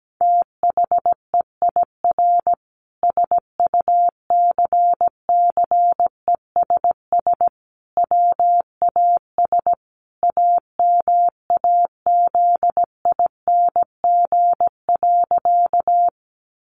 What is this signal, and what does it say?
Morse code, heard as a keyed tone, about 17 wpm, 700 Hz. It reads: THEIR SUCCESS WAS AMAZING.
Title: Sentences from Top 1500 Words 17wpm